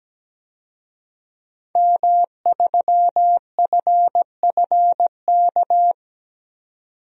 Morse code, heard as a keyed tone, about 17 wpm, 700 Hz. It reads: M3FFK
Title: DXCC-100 Call Signs - 5 Characters - 17wpm